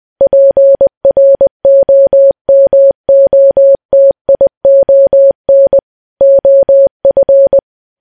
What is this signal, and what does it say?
Morse code, heard as a keyed tone, about 20 wpm, 550 Hz. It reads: PROMOTION OF
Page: news-headlines-in-morse-code-at-20-wpm